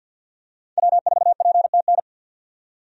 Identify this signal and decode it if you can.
Morse code, heard as a keyed tone, about 53 wpm, 700 Hz. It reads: W4PTN
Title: USA Call Signs - 5 Characters - 50wpm